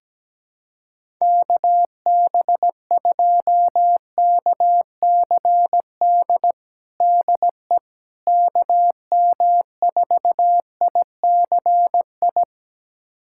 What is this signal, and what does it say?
Morse code, heard as a keyed tone, about 17 wpm, 700 Hz. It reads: KB2KCD DE KM4ICI
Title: USA Callsign DE Callsign 17wpm